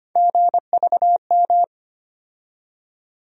Morse code, heard as a keyed tone, about 25 wpm, 700 Hz. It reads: GVM
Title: USA Call Sign Postfixes - 3 Letters 25wpm